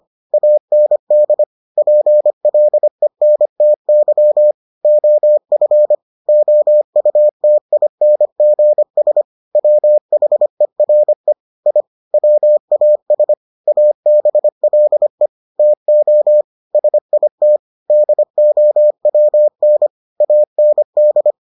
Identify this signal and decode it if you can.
Morse code, heard as a keyed tone, about 25 wpm, 600 Hz. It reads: AND PLENTY OF OUTINGS WHERE I WAS ABLE TO SIT DOWN AND